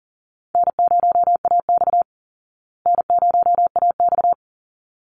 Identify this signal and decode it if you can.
Morse code, heard as a keyed tone, about 40 wpm, 700 Hz. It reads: N0AX N0AX
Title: Easy Call Signs Repeated 2x - 40wpm